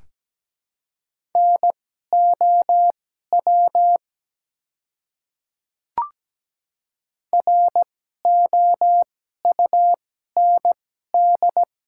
Morse code, heard as a keyed tone, about 17 wpm, spaced as for 12 wpm, 700 Hz. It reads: NOW ROUND
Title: 2x Character Spacing - Top 200 Words - 17wpm